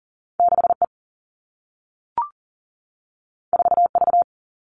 Morse code, heard as a keyed tone, about 41 wpm, 700 Hz. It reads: 6E 4V
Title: All Cut Numbers - 2-Digit 40wpm